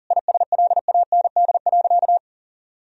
Morse code, heard as a keyed tone, about 40 wpm, 700 Hz. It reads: ISLAND.
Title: Sentences from Top 1000 Words - Review Part 3 - 40wpm